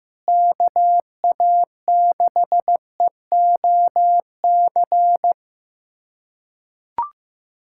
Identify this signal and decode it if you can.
Morse code, heard as a keyed tone, about 16 wpm, 700 Hz. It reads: KA6EOC
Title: USA Callsign DE Callsign 15wpm